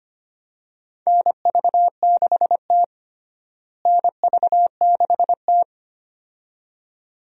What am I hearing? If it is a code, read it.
Morse code, heard as a keyed tone, about 25 wpm, 700 Hz. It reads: NV6T NV6T
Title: USA Call Signs - 4 Characters - Repeated 2x - 25wpm